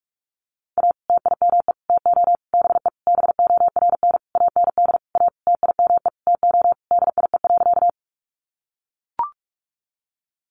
Morse code, heard as a keyed tone, about 44 wpm, 700 Hz. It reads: A TIME TO BE BORN AND A TIME TO DIE.
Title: Sentences from Top 1300 Words 45wpm